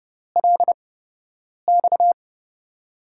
Morse code, heard as a keyed tone, about 32 wpm, spaced as for 8 wpm, 700 Hz. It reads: LX
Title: USA Call Signs - 6 Characters - 8x Character Spacing - 30wpm